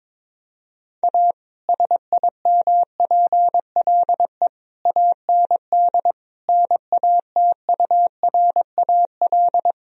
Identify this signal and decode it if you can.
Morse code, heard as a keyed tone, about 22 wpm, 700 Hz. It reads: A SIMPLE AND NATURAL
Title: Sentences from Top 1000 Words - Review Part 4 - 22wpm